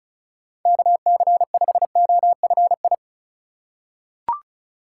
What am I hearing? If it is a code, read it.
Morse code, heard as a keyed tone, about 36 wpm, 700 Hz. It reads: KC5OFI
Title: USA Call Signs - 6 Characters - 4x Character Spacing - 35wpm